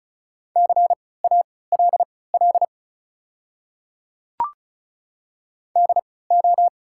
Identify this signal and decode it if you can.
Morse code, heard as a keyed tone, about 36 wpm, spaced as for 20 wpm, 700 Hz. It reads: CALL DO
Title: 3x Character Spacing - Top 100 Words - 35wpm